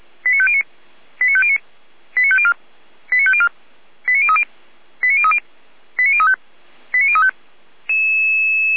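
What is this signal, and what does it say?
Five-tone selective call (ZVEI), heard as quick runs of five tones, tones of about 70 ms. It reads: 89509 89509 89504 89504 89030 89030 89035 89035
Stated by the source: ZVEI five-tone selective call received by SDR, from the Feuerwehr Orlamünde audio server.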